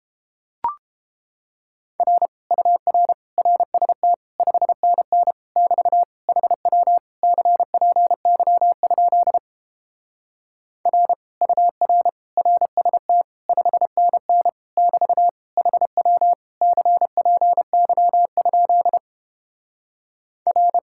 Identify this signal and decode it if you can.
Morse code, heard as a keyed tone, about 31 wpm, 700 Hz. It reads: R UR RST 5NN <BT> HW CPY? R UR RST 5NN <BT> HW CPY? R